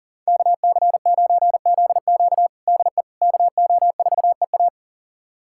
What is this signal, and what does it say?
Morse code, heard as a keyed tone, about 40 wpm, 700 Hz. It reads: KC9ZQ DE KO4EA
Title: USA Callsign DE Callsign - 5 Characters - 40wpm